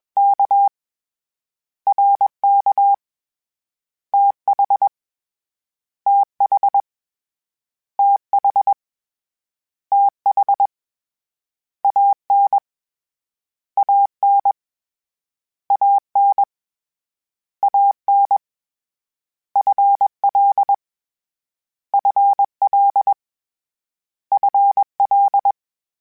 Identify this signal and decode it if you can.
Morse code, heard as a keyed tone, about 20 wpm, 800 Hz. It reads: K R K T H T H T H T H A N A N A N A N F L F L F L
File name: Contrasts / Contrasts-mp3s / 2LtrContrasts @20